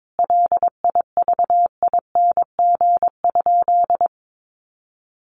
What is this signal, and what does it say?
Morse code, heard as a keyed tone, about 22 wpm, 700 Hz. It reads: LIVING?
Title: Sentences from Top 1100 Words 22wpm